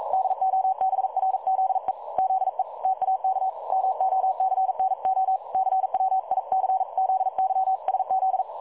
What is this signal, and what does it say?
Morse code, heard as a keyed tone, about 42 wpm, 760 Hz. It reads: DOBRZE ZE TAK MOCNO CWICZYSZ
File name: dlazaawansowanych / nagraniazpasma / websdr recording 3565.0kHz